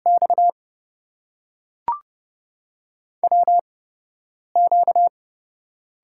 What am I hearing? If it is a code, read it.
Morse code, heard as a keyed tone, about 30 wpm, spaced as for 5 wpm, 700 Hz. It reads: XWQ
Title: USA Call Signs - 4 Characters - 8x Character Spacing - 30wpm